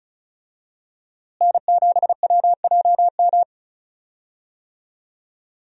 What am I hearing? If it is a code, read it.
Morse code, heard as a keyed tone, about 36 wpm, 700 Hz. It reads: N7WJM
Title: USA Call Signs - 5 Characters - Repeated 2x - 35wpm